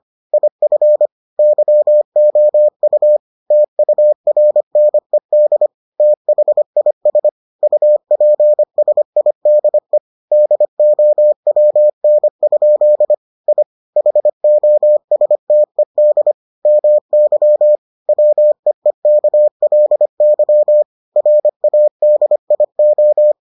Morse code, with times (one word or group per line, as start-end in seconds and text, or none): 0.3-1.1 IF
1.4-3.2 YOU
3.5-5.7 TURNED
6.0-7.3 THIS
7.6-10.0 UPSIDE
10.3-13.1 DOWN?
13.5-13.6 I
14.0-16.3 HOSTED
16.7-17.8 MY
18.1-20.8 WEEKLY
21.2-23.3 RADIO